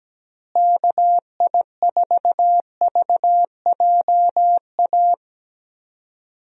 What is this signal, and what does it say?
Morse code, heard as a keyed tone, about 17 wpm, 700 Hz. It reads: KI4VJA
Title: USA Call Signs - 6 Characters - Repeated 2x - 17wpm